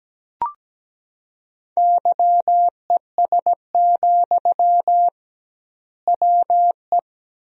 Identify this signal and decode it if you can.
Morse code, heard as a keyed tone, about 17 wpm, 700 Hz. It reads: YES, WE
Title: 2x Word-Spacing - Sentences from Top 500 Words - 17wpm